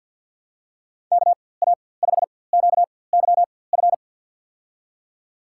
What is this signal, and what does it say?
Morse code, heard as a keyed tone, about 50 wpm, spaced as for 23 wpm, 700 Hz. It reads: KA5QYF
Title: USA Call Signs - 6 Characters - 4x Character Spacing - 50wpm